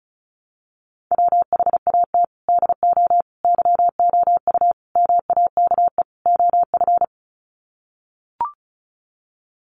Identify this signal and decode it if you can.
Morse code, heard as a keyed tone, about 34 wpm, 700 Hz. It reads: WHAT DO YOU MAKE OF
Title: Sets of 5 Words 35wpm